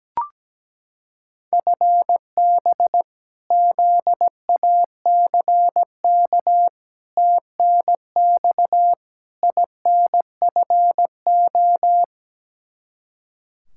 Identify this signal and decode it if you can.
Morse code, heard as a keyed tone, about 17 wpm, 700 Hz. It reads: FB ZACK TNX INFO
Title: Practice QSO - Exchange 3 - More Info - 17wpm